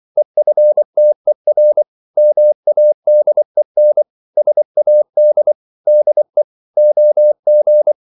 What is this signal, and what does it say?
Morse code, heard as a keyed tone, about 24 wpm, 600 Hz. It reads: EFTER MADEN SAD DE OG
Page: Ferie 24 wpm